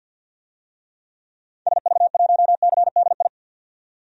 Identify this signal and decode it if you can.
Morse code, heard as a keyed tone, about 52 wpm, 700 Hz. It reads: SV1CDI
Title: DXCC-100 Call Signs - 6 Characters - 50wpm